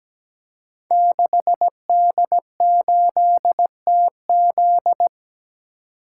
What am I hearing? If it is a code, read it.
Morse code, heard as a keyed tone, about 17 wpm, 700 Hz. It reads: 6D8TZ
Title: Code Groups - 5 Letter:Numbers - 17wpm